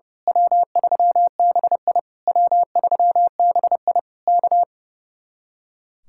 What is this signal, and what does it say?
Morse code, heard as a keyed tone, about 28 wpm, 700 Hz. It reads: W3BI W3BI K
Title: Speed-Racing Common QSO Phrases - 30wpm